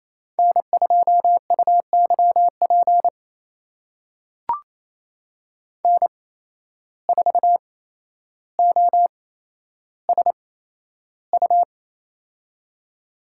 Morse code, heard as a keyed tone, about 28 wpm, spaced as for 12 wpm, 700 Hz. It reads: N2UYP N4OSU
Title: USA Call Signs - 5 Characters - 8x Character Spacing - 28wpm